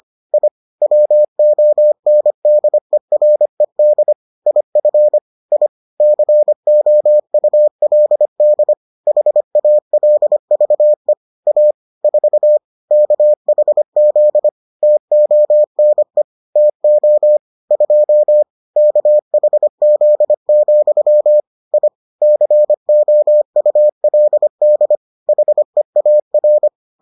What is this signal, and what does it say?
Morse code, heard as a keyed tone, about 25 wpm, 600 Hz. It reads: I WONDERED IF I COULD HALVE A 4 KHZ TONE TO 2 KHZ, I COULD HEAR